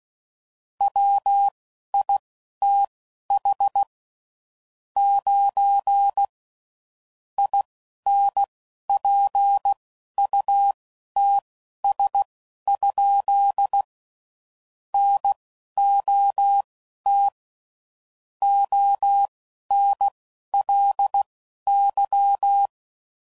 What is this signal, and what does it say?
Morse code, heard as a keyed tone, about 16 wpm, spaced as for 11 wpm, 800 Hz. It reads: WITH 9 INPUTS? NOT ONLY
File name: CW Code Groups @10WPM / Technical Subjects / AND Gate